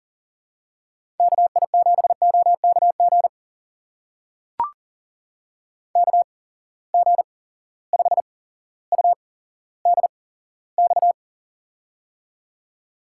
Morse code, heard as a keyed tone, about 40 wpm, spaced as for 17 wpm, 700 Hz. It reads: KI7OKG KG5UDX
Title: USA Call Signs - 6 Characters - 8x Character Spacing - 40wpm